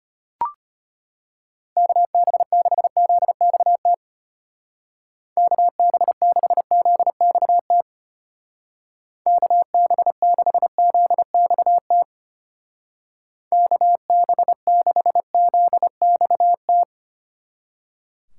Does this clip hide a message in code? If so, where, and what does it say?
Yes: Morse code from 1.8 to 16.8 s, heard as a keyed tone, about 33 wpm, 700 Hz. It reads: KB6ZXT KB6ZXT KB6ZXT KB6ZXT